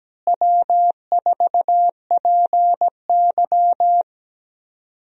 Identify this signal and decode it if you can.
Morse code, heard as a keyed tone, about 17 wpm, 700 Hz. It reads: W4PY
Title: USA Call Signs - 4 Characters - Repeated 3x - 17wpm